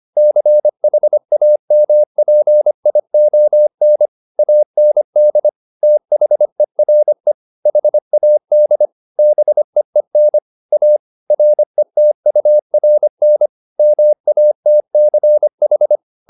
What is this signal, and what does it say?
Morse code, heard as a keyed tone, about 25 wpm, 600 Hz. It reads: CHAMPION AND THERE HAD BEEN A RETURN MATCH